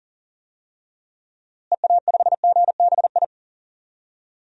Morse code, heard as a keyed tone, about 40 wpm, 700 Hz. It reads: EA5GBI
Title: DXCC-100 Call Signs - 6 Characters - Repeated 3x - 40wpm